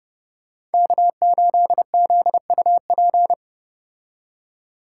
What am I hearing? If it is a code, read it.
Morse code, heard as a keyed tone, about 31 wpm, 700 Hz. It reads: K8ZUP